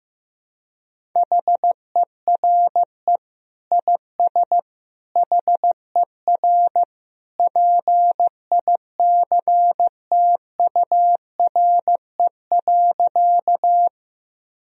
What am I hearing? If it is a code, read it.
Morse code, heard as a keyed tone, about 15 wpm, 700 Hz. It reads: HERE IS HER PICTURE.
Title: Sentences from Top 1000 Words - Review Part 1 - 15wpm